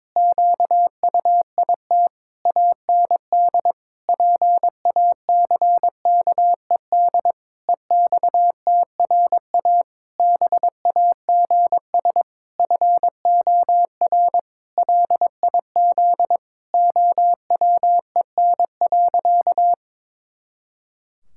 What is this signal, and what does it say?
Morse code, heard as a keyed tone, about 22 wpm, 700 Hz. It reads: QUIT AND PACKED EXTRA BAGS FOR LIZ OWEN.
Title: Pangram Challenge v2 - 22wpm